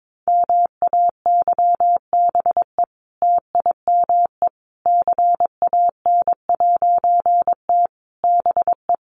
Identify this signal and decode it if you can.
Morse code, heard as a keyed tone, about 22 wpm, 700 Hz. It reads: MAYBE TIME CAN'T BE
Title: A Sound Of Thunder 22wpm